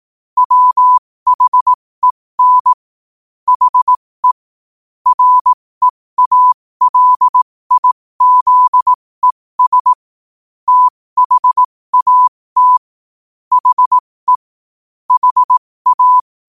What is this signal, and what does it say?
Morse code, heard as a keyed tone, about 18 wpm, spaced as for 15 wpm, 1000 Hz. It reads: WHEN HE REALIZES THAT HE HA
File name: qotdfiles / 15wpm